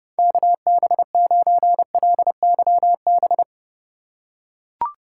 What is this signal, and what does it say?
Morse code, heard as a keyed tone, about 30 wpm, 700 Hz. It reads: KB9LYB
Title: USA Call Signs - 6 Characters - 2x Character Spacing - 30wpm